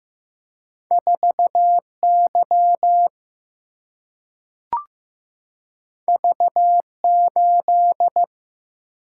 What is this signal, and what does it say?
Morse code, heard as a keyed tone, about 15 wpm, 700 Hz. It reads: 4 Y V 8
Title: Single-Letter Single-Number 15wpm